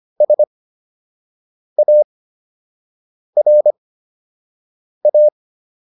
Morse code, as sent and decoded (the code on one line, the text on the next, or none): ... .- .-. .-
SARA